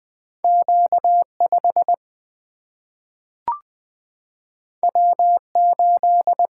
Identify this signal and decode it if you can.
Morse code, heard as a keyed tone, about 20 wpm, 700 Hz. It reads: Q5 W8
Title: Single-Letter Single-Number 20wpm